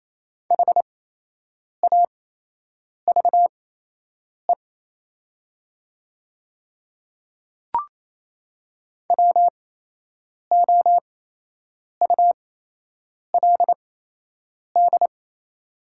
Morse code, heard as a keyed tone, about 28 wpm, spaced as for 8 wpm, 700 Hz. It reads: HAVE WOULD
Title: 8x Character Spacing - Top 100 Words - 28wpm